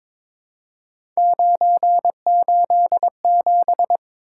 Morse code, heard as a keyed tone, about 22 wpm, 700 Hz. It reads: 987
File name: practice / 22wpm / Three Letter-Number Combinations - 22wpm